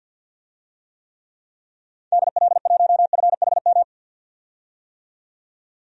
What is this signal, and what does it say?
Morse code, heard as a keyed tone, about 51 wpm, 700 Hz. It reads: DL1FHM
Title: DXCC-100 Call Signs - 6 Characters - 50wpm